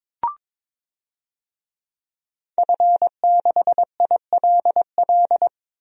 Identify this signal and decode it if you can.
Morse code, heard as a keyed tone, about 21 wpm, 700 Hz. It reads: F6ILL
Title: DXCC-100 Call Signs - 5 Characters - Repeated 2x - 22wpm